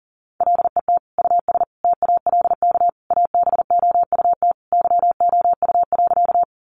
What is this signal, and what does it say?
Morse code, heard as a keyed tone, about 40 wpm, 700 Hz. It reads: LET US TALK ABOUT YOU.
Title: Sentences from Top 400 Words 40wpm